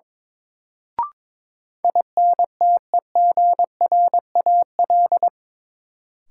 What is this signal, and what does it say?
Morse code, heard as a keyed tone, about 22 wpm, 700 Hz. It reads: INTEGRAL
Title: Sets of 5 Words - New Word Drill - 22wpm